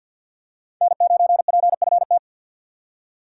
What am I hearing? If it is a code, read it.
Morse code, heard as a keyed tone, about 52 wpm, 700 Hz. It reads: N9PFT
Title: USA Call Signs - 5 Characters - 4x Character Spacing - 50wpm